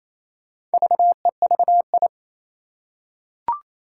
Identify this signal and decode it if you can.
Morse code, heard as a keyed tone, about 28 wpm, 700 Hz. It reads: VEVI
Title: Four Challenging Letter-Number Combinations 28wpm